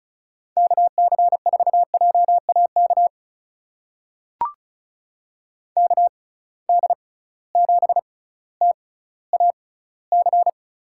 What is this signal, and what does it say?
Morse code, heard as a keyed tone, about 36 wpm, spaced as for 18 wpm, 700 Hz. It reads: KC4JAK KD7TAC
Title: USA Call Signs - 6 Characters - 6x Character Spacing - 35wpm